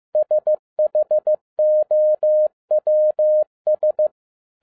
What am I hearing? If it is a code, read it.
Morse code, heard as a keyed tone, about 15 wpm, 610 Hz. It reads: SHOWS
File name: 2012-01-mp3 / n07-ARA JR0DIL-15wpm